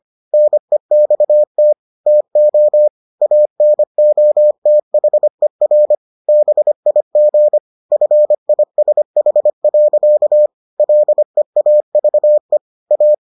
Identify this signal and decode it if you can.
Morse code, heard as a keyed tone, about 25 wpm, 600 Hz. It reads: NEXT TO ANOTHER BIG FISH. LEAVE A